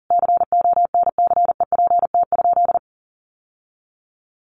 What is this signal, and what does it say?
Morse code, heard as a keyed tone, about 40 wpm, 700 Hz. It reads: CONCEPT?
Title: Sentences from Top 1100 Words 40wpm